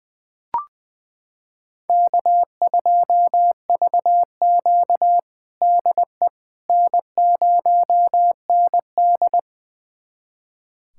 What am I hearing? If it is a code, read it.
Morse code, heard as a keyed tone, about 20 wpm, 700 Hz. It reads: K2VQ DE N0ND
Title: USA Callsign DE Callsign - 4 Characters - 20wpm